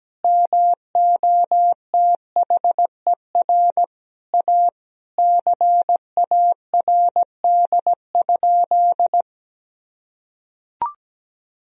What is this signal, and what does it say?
Morse code, heard as a keyed tone, about 17 wpm, 700 Hz. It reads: MOTHER A CARD?
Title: Sentences from Top 1000 Words - Review Part 4 - 17wpm